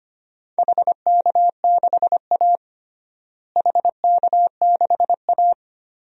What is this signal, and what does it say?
Morse code, heard as a keyed tone, about 25 wpm, 700 Hz. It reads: HK6A HK6A
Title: DXCC-100 Call Signs - 4 Characters - Repeated 2x - 25wpm